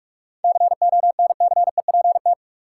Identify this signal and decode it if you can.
Morse code, heard as a keyed tone, about 46 wpm, 700 Hz. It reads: CONCEPT
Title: Sentences from Top 1100 Words - New Word Drill Alt - 45wpm